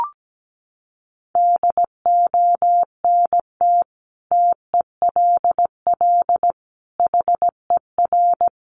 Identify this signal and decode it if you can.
Morse code, heard as a keyed tone, about 17 wpm, 700 Hz. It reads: DONT TELL HER
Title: Sentences from Top 700 Words 17wpm